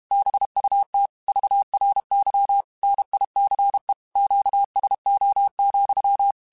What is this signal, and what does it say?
Morse code, heard as a keyed tone, about 32 wpm, 800 Hz. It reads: BUT VRY NICE QSO,
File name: QSOs / QSOs-mp3s / WA7QOX 2@30